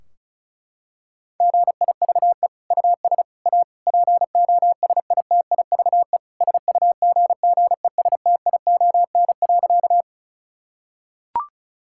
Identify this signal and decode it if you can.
Morse code, heard as a keyed tone, about 34 wpm, 700 Hz. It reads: GIVE US A POSITIVE SUGGESTION.
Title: Sentences from Top 1600 Words 35wpm